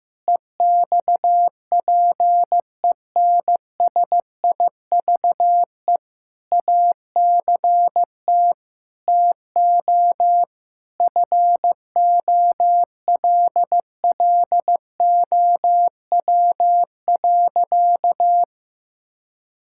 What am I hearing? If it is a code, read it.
Morse code, heard as a keyed tone, about 16 wpm, 700 Hz. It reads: EXPENSIVE ACT TO FOLLOW.
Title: Sentences from Top 1000 Words - Review Part 6 - 15wpm